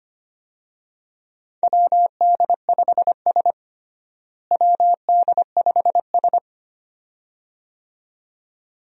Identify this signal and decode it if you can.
Morse code, heard as a keyed tone, about 25 wpm, 700 Hz. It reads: WD5S WD5S
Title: USA Call Signs - 4 Characters - Repeated 2x - 25wpm